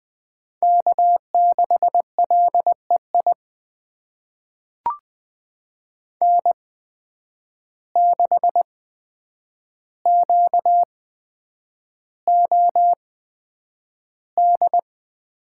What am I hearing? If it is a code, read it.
Morse code, heard as a keyed tone, about 20 wpm, spaced as for 9 wpm, 700 Hz. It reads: K6LEI N6QOD
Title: USA Call Signs - 5 Characters - 8x Character Spacing - 20wpm